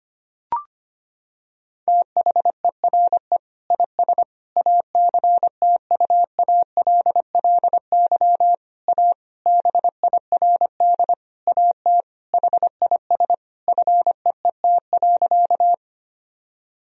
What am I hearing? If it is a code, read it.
Morse code, heard as a keyed tone, about 25 wpm, 700 Hz. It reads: THERE IS ACTUALLY A BIRD AT HIS FEET.
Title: Sentences from Top 1000 Words - Review Part 4 - 25wpm